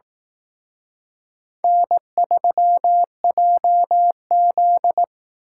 Morse code, heard as a keyed tone, about 18 wpm, 700 Hz. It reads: N3JZ
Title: Speed Racing - USA Call Signs 15wpm